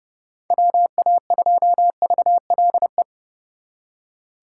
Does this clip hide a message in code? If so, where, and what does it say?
Yes: Morse code from 0.5 to 3.0 s, heard as a keyed tone, about 40 wpm, 700 Hz. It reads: WA2VLE